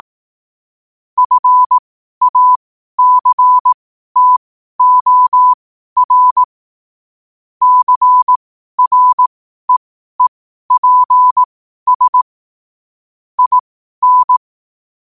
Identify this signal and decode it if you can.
Morse code, heard as a keyed tone, about 18 wpm, spaced as for 12 wpm, 1000 Hz. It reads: FACTOR CREEPS IN
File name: qotdfiles / 12wpm